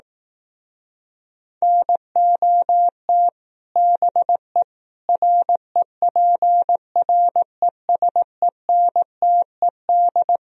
Morse code, heard as a keyed tone, about 18 wpm, 700 Hz. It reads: NOT BE REPRESENTED
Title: Speed-Racing - Sets of 5 Words 15wpm